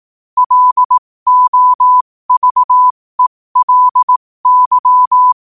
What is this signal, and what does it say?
Morse code, heard as a keyed tone, about 18 wpm, spaced as for 15 wpm, 1000 Hz. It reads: LOVELY
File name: qotdfiles / 15wpm